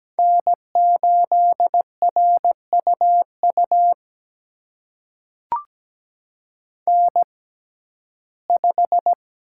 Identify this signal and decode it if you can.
Morse code, heard as a keyed tone, about 17 wpm, spaced as for 12 wpm, 700 Hz. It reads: N8RUU N5
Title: USA Call Signs - 6x Character Spacing - 17wpm